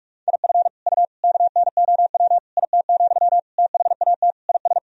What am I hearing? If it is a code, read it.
Morse code, heard as a keyed tone, about 46 wpm, 700 Hz. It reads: IF U KNOW IT, THAT IS
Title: Sentences from Top 400 Words and Abbreviations 45wpm